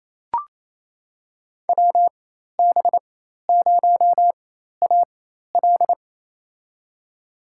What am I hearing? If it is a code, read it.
Morse code, heard as a keyed tone, about 27 wpm, spaced as for 13 wpm, 700 Hz. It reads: WB0AL
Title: USA Call Signs - 5 Characters - 4x Character Spacing - 28wpm